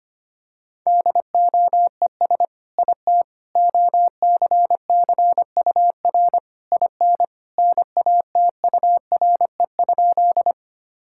Morse code, heard as a keyed tone, about 25 wpm, 700 Hz. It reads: DOES IT OCCUR IN NATURE?
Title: Sentences from Top 1000 Words - Review Part 6 - 25wpm